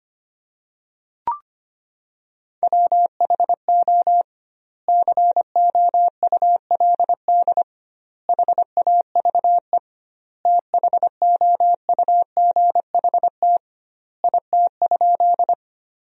Morse code, heard as a keyed tone, about 25 wpm, 700 Hz. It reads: WHO COULD HAVE THOUGHT IT?
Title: 2x Word-Spacing - Sentences from Top 300 Words - 25wpm